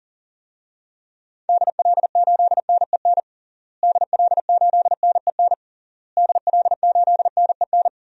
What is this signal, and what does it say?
Morse code, heard as a keyed tone, about 37 wpm, 700 Hz. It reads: DL8NEN DL8NEN DL8NEN
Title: DXCC-100 Call Signs - 6 Characters - Repeated 3x - 40wpm